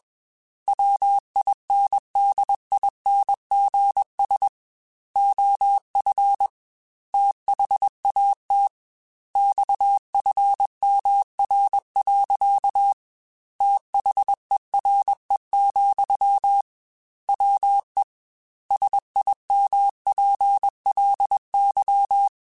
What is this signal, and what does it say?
Morse code, heard as a keyed tone, about 21 wpm, 800 Hz. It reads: WINDINGS OF THAT XFMR. THERE, WE SIMPLY
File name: Technical Subjects / Technical Subjects-mp3s / Sw Pwr Sup @20